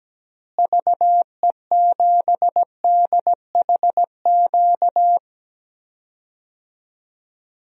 Morse code, heard as a keyed tone, about 17 wpm, 700 Hz. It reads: VE7DHQ
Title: DXCC-100 Call Signs - 6 Characters - Repeated 3x - 17wpm